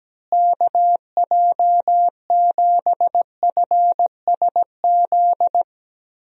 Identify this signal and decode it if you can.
Morse code, heard as a keyed tone, about 17 wpm, 700 Hz. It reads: KJ7FSZ
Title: USA Call Signs - 6 Characters - Repeated 2x - 17wpm